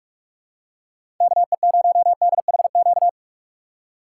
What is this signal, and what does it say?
Morse code, heard as a keyed tone, about 46 wpm, 700 Hz. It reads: KE0DHQ
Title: USA Call Signs - 6 Characters - 2x Character Spacing - 45wpm